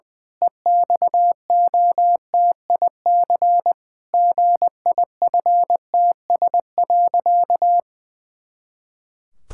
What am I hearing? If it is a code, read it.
Morse code, heard as a keyed tone, about 20 wpm, 700 Hz. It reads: EXOTIC GIFTS.
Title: Pangram Challenge v2 - 20wpm